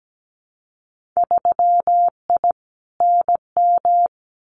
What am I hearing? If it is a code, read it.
Morse code, heard as a keyed tone, about 17 wpm, 700 Hz. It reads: 3I NM
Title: Winter Field Day Exchanges Encore 17wpm